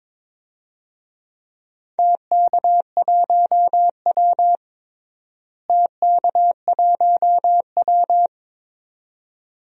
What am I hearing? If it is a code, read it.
Morse code, heard as a keyed tone, about 22 wpm, 700 Hz. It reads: TK1W TK1W
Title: DXCC-100 Call Signs - 4 Characters - Repeated 2x - 22wpm